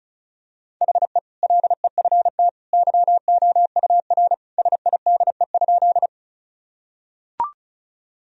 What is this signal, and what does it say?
Morse code, heard as a keyed tone, about 36 wpm, 700 Hz. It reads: HE LEFT YOUR SIDE?
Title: Sentences from Top 1000 Words - Review Part 1 - 35wpm